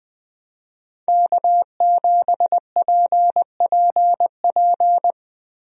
Morse code, heard as a keyed tone, about 21 wpm, 700 Hz. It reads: K7PPP
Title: USA Call Signs - 5 Characters - Repeated 2x - 20wpm